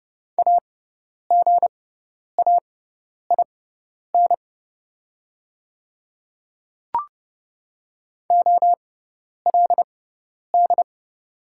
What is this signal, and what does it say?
Morse code, heard as a keyed tone, about 30 wpm, spaced as for 10 wpm, 700 Hz. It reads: AGAIN OLD
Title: 6x Character Spacing - Top 300 Words - 30wpm